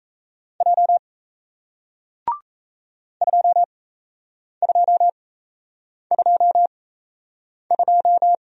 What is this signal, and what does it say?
Morse code, heard as a keyed tone, about 37 wpm, spaced as for 8 wpm, 700 Hz. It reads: J 2222